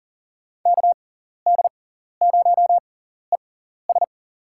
Morse code, heard as a keyed tone, about 40 wpm, spaced as for 14 wpm, 700 Hz. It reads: KD0ES
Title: USA Call Signs - 5 Characters - 6x Character Spacing - 40wpm